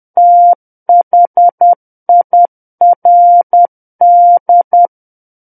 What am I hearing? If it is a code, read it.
Morse code, heard as a keyed tone, about 10 wpm, 700 Hz. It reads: THIRD
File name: Book One / 01-Chapter 1 - 10 WPM